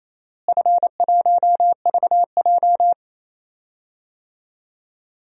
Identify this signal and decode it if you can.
Morse code, heard as a keyed tone, about 28 wpm, 700 Hz. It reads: F1VJ
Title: DXCC-100 Call Signs - 4 Characters - Repeated 3x - 28wpm